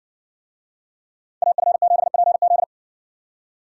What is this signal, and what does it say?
Morse code, heard as a keyed tone, about 60 wpm, 700 Hz. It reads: AF7PZ